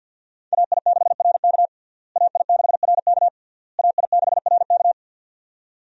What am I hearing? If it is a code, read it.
Morse code, heard as a keyed tone, about 48 wpm, 700 Hz. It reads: AI6RK AI6RK AI6RK
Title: USA Call Signs - 5 Characters - Repeated 3x - 50wpm